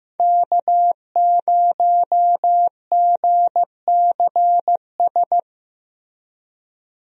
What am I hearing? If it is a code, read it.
Morse code, heard as a keyed tone, about 15 wpm, 700 Hz. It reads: K0GCS
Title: USA Call Signs - 5 Characters - Repeated 2x - 15wpm